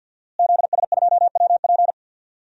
Morse code, heard as a keyed tone, about 46 wpm, 700 Hz. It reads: ZS2WP
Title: DXCC-100 Call Signs - 5 Characters - Repeated 3x - 50wpm